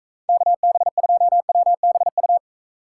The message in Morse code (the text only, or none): KB2WBU